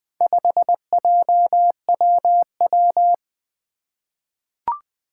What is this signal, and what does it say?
Morse code, heard as a keyed tone, about 20 wpm, 700 Hz. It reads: 5JWW
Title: Four Challenging Letter-Number Combinations 20wpm